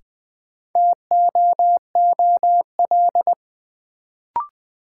Morse code, heard as a keyed tone, about 20 wpm, 700 Hz. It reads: TOOL
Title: Sentences from Top 1000 Words - Review Part 6 - New Word Drill - 20wpm